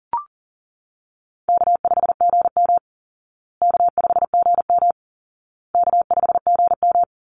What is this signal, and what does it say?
Morse code, heard as a keyed tone, about 38 wpm, 700 Hz. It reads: K5GM K5GM K5GM
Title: Easy Call Signs Repeated 3x - 40wpm